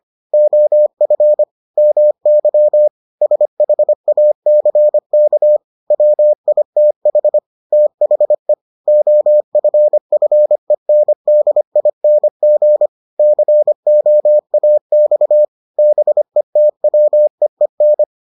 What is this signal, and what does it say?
Morse code, heard as a keyed tone, about 25 wpm, 600 Hz. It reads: OF MY SHACK WITH THE OFFENDING COAX BETWEEN